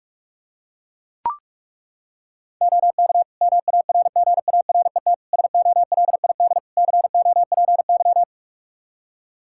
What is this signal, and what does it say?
Morse code, heard as a keyed tone, about 42 wpm, 700 Hz. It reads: OK MARGARET SOLID COPY
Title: Practice QSO - Exchange 3 - More Info - 45wpm